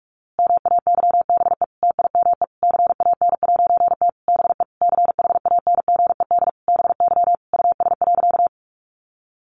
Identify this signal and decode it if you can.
Morse code, heard as a keyed tone, about 44 wpm, 700 Hz. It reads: MAYBE TIME CAN'T BE CHANGED BY US.
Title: A Sound Of Thunder 45wpm